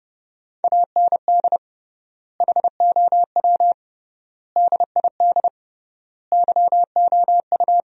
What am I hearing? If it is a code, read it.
Morse code, heard as a keyed tone, about 30 wpm, 700 Hz. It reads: AND HOW DID YOU
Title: 3x Word-Spacing - Sentences from Top 200 Words - 30wpm